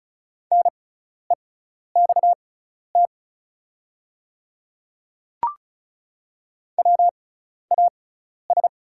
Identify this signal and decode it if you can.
Morse code, heard as a keyed tone, about 36 wpm, spaced as for 12 wpm, 700 Hz. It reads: NEXT WAS
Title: 6x Character Spacing - Top 300 Words - 35wpm